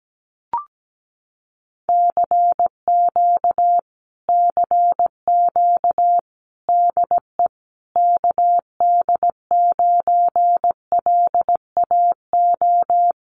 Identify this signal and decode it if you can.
Morse code, heard as a keyed tone, about 17 wpm, 700 Hz. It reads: CQ CQ DE KD9LAO
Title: Common QSO Phrases Encore 17wpm